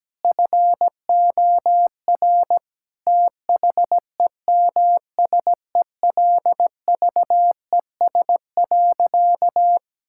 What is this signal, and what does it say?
Morse code, heard as a keyed tone, about 17 wpm, 700 Hz. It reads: FOR THEMSELVES.
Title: Sentences from Top 1000 Words - Review Part 5 - 17wpm